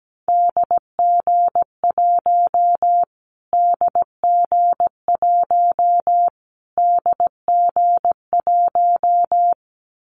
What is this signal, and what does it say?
Morse code, heard as a keyed tone, about 17 wpm, 700 Hz. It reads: DG1 DG1 DG1
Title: Callsign Prefixes for Top 25 DXCC - Repeated 3x - 17wpm